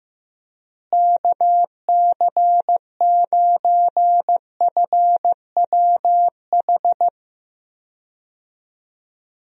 Morse code, heard as a keyed tone, about 15 wpm, 700 Hz. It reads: KC9FWH
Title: USA Call Signs - 6 Characters - Repeated 2x - 15wpm